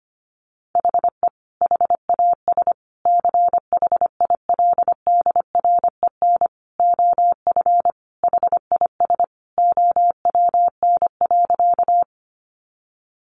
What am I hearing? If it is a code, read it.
Morse code, heard as a keyed tone, about 25 wpm, 700 Hz. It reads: HE HAS CHILDREN OF HIS OWN.
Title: Sentences from Top 300 Words 25wpm